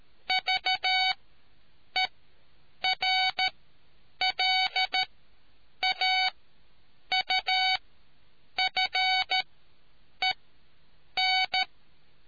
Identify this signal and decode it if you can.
Morse code, heard as a keyed tone, about 13 wpm, spaced as for 8 wpm, 750 Hz. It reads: VERLAUFEN